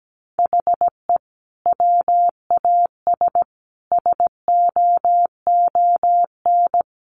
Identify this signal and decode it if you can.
Morse code, heard as a keyed tone, about 17 wpm, 700 Hz. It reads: HE WAS SOON
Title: Sentences from Top 1200 Words 17wpm